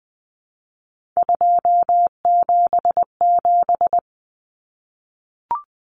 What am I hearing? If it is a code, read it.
Morse code, heard as a keyed tone, about 20 wpm, 700 Hz. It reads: 277
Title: Common Cut Numbers - 3-Digit 20wpm